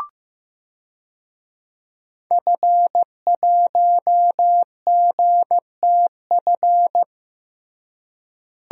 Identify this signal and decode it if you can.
Morse code, heard as a keyed tone, about 15 wpm, 700 Hz. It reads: F1GTF
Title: DXCC-100 Call Signs - 5 Characters - Repeated 2x - 15wpm